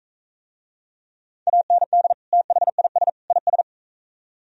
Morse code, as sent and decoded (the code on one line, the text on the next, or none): .- -. -.. / - .... .. ... / .. ...
AND THIS IS